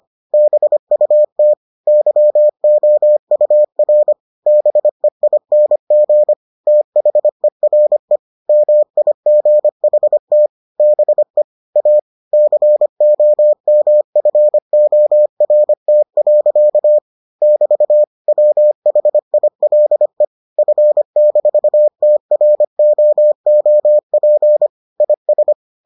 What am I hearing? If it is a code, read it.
Morse code, heard as a keyed tone, about 25 wpm, 600 Hz. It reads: BUT YOUR BEING THERE MIGHT BE A COMFORT. <BT> WHILE F-TROOP IS